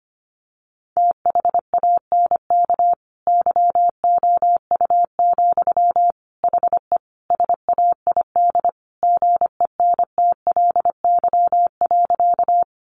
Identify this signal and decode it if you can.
Morse code, heard as a keyed tone, about 25 wpm, 700 Hz. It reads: THANK YOU, HE SAID GENTLY.
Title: Sentences from Top 1500 Words 25wpm